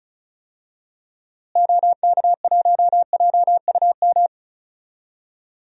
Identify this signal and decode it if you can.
Morse code, heard as a keyed tone, about 36 wpm, 700 Hz. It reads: OK1JUM
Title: DXCC-100 Call Signs - 6 Characters - Repeated 2x - 35wpm